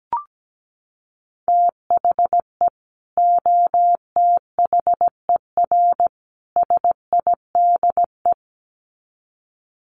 Morse code, heard as a keyed tone, about 17 wpm, 700 Hz. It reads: THE OTHER SIDE
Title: Sets of 3 Words 17wpm